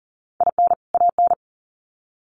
Morse code, heard as a keyed tone, about 40 wpm, 700 Hz. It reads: IN AN
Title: Sets of 2 Words 40wpm